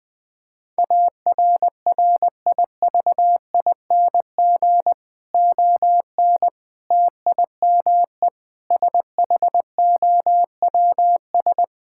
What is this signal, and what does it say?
Morse code, heard as a keyed tone, about 20 wpm, 700 Hz. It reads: ARRIVING ON TIME SHOWS
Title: Difficult Sentences with Hints - 20wpm